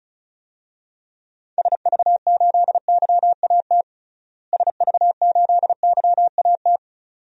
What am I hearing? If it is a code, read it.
Morse code, heard as a keyed tone, about 36 wpm, 700 Hz. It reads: SV8YAT SV8YAT
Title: DXCC-100 Call Signs - 6 Characters - Repeated 2x - 35wpm